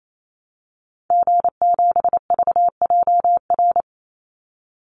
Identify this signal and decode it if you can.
Morse code, heard as a keyed tone, about 28 wpm, 700 Hz. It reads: G7VJR
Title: Call Signs Repeated 2x - 28wpm